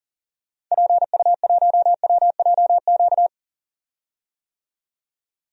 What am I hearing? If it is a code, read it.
Morse code, heard as a keyed tone, about 40 wpm, 700 Hz. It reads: PU1WJQ
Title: DXCC-100 Call Signs - 6 Characters - 40wpm